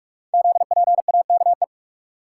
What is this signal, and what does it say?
Morse code, heard as a keyed tone, about 44 wpm, 690 Hz. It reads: ZPAKE
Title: Code Groups - 5 Letter:Numbers - 45wpm